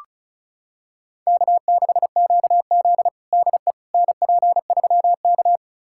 Morse code, heard as a keyed tone, about 36 wpm, 700 Hz. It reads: K6QZ DE NP3K
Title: USA Callsign DE Callsign - 4 Characters - 35wpm